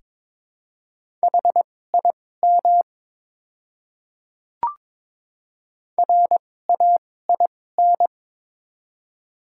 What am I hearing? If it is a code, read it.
Morse code, heard as a keyed tone, about 21 wpm, spaced as for 16 wpm, 700 Hz. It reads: HIM RAIN
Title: 2x Character Spacing - Top 300 Words - 22wpm